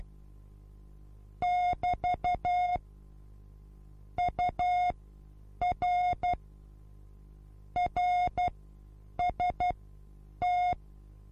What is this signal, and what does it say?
Morse code, heard as a keyed tone, about 12 wpm, spaced as for 8 wpm, 720 Hz. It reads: <BT> UR RST
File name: repos / Morsekurs / DJ4UF Morsen23